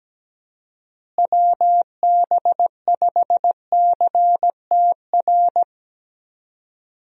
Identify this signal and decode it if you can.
Morse code, heard as a keyed tone, about 17 wpm, 700 Hz. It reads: WB5CTR
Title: USA Call Signs - 6 Characters - Repeated 2x - 17wpm